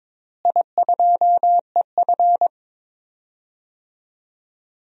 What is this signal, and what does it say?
Morse code, heard as a keyed tone, about 22 wpm, 700 Hz. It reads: I2EF
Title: DXCC-100 Call Signs - 4 Characters - Repeated 3x - 22wpm